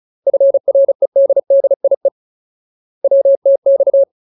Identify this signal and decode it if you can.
Morse code, heard as a keyed tone, about 34 wpm, 550 Hz. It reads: FREDDIE WTX
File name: Ultimate Word List / 35WPM Ultimate